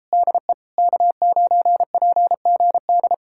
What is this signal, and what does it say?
Morse code, heard as a keyed tone, about 38 wpm, 700 Hz. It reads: DE K9PGD